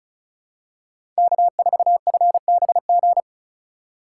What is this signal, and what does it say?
Morse code, heard as a keyed tone, about 34 wpm, 700 Hz. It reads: K4FBG
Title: USA Call Signs - 5 Characters - 4x Character Spacing - 35wpm